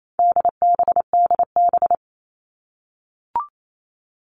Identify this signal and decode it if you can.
Morse code, heard as a keyed tone, about 28 wpm, 700 Hz. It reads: DBDB
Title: All Cut Numbers - 4-Digit 28wpm